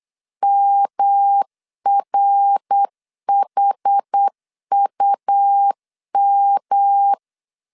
Morse code, heard as a keyed tone, about 8 wpm, 800 Hz. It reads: MRHUM
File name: CW / DET / 1 6 1500C S B 8wpm 5142